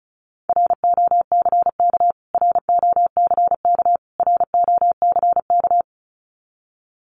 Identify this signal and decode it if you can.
Morse code, heard as a keyed tone, about 34 wpm, 700 Hz. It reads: ROCK ROCK ROCK
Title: Top 500 Words Repeated 35wpm